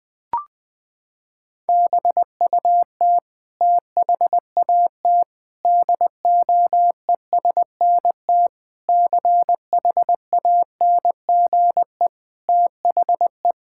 Morse code, heard as a keyed tone, about 20 wpm, 700 Hz. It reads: BUT THAT DOESNT CHANGE THE
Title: Sentences with 7-Word Phrases - 20wpm